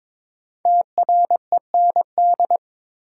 Morse code, heard as a keyed tone, about 22 wpm, 700 Hz. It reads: TREND
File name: 5 Letter Words - First 3 Letters Given - 22wpm